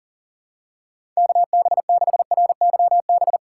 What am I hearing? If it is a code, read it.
Morse code, heard as a keyed tone, about 40 wpm, 700 Hz. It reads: KB6RYB
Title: USA Call Signs - 6 Characters - 6x Character Spacing - 40wpm